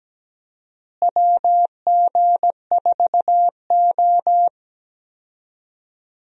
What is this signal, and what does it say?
Morse code, heard as a keyed tone, about 17 wpm, 700 Hz. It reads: WG4O
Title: USA Call Signs - 4 Characters - Repeated 2x - 17wpm